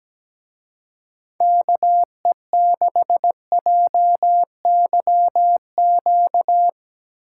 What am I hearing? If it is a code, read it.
Morse code, heard as a keyed tone, about 18 wpm, 700 Hz. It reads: KE6JYQ
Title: Speed Racing - USA Call Signs 17wpm